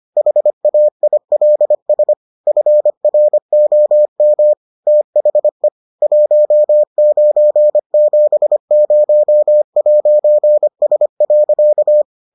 Morse code, heard as a keyed tone, about 25 wpm, 600 Hz. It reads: HAILS FROM THE 1970'S.